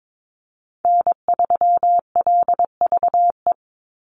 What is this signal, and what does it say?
Morse code, heard as a keyed tone, about 22 wpm, 700 Hz. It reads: N3LVE
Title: USA Call Signs 22wpm